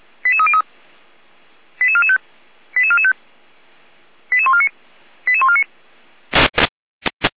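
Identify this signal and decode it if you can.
Five-tone selective call (ZVEI), heard as quick runs of five tones, tones of about 70 ms. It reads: 88303 88445 88445 88149 88149
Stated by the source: ZVEI five-tone call received by SDR, archived by fire brigade Feuerwehr Orlamünde.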